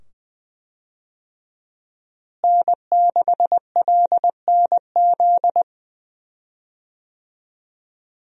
Morse code, heard as a keyed tone, about 20 wpm, 700 Hz. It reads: N6LNZ
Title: USA Call Signs - 5 Characters - Repeated 2x - 20wpm